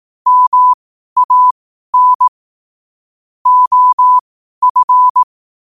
Morse code, heard as a keyed tone, about 18 wpm, spaced as for 12 wpm, 1000 Hz. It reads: MAN OF
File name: qotdfiles / 12wpm